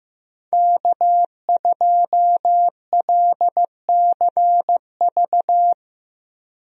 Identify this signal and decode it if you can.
Morse code, heard as a keyed tone, about 15 wpm, 700 Hz. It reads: K2LCV
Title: USA Call Signs - 5 Characters - Repeated 2x - 15wpm